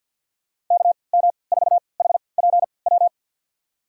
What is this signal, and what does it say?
Morse code, heard as a keyed tone, about 46 wpm, spaced as for 27 wpm, 700 Hz. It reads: KM4HPW
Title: USA Call Signs - 6 Characters - 3x Character Spacing - 50wpm